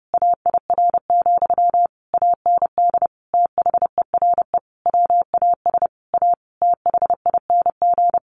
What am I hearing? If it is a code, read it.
Morse code, heard as a keyed tone, about 30 wpm, 700 Hz. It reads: AIR, AND THERE WAS A THING
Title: A Sound Of Thunder 30wpm